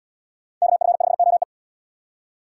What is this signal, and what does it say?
Morse code, heard as a keyed tone, about 75 wpm, 700 Hz. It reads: KF5OE